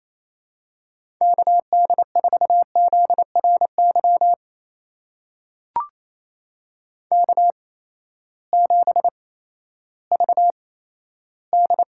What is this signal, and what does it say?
Morse code, heard as a keyed tone, about 28 wpm, spaced as for 14 wpm, 700 Hz. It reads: KD4ZRY K7VD
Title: USA Call Signs - 8x Character Spacing - 28wpm